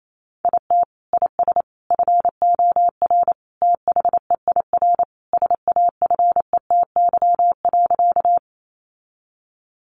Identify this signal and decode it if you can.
Morse code, heard as a keyed tone, about 28 wpm, 700 Hz. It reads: IT IS FOR THEIR SAFETY.
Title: Sentences from Top 700 Words 28wpm